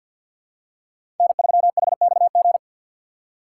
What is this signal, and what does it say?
Morse code, heard as a keyed tone, about 51 wpm, 700 Hz. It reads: N3HXG
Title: USA Call Signs - 5 Characters - 6x Character Spacing - 50wpm